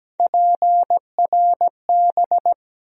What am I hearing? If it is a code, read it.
Morse code, heard as a keyed tone, about 17 wpm, 700 Hz. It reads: PRB
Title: USA Call Sign Postfixes - 3 Letters 17wpm